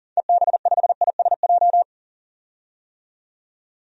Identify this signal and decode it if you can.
Morse code, heard as a keyed tone, about 40 wpm, 700 Hz. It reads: EB5ISJ
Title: DXCC-100 Call Signs - 6 Characters - 40wpm